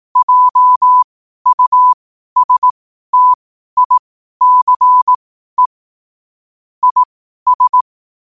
Morse code, heard as a keyed tone, about 18 wpm, spaced as for 12 wpm, 1000 Hz. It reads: JUSTICE IS
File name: qotdfiles / 12wpm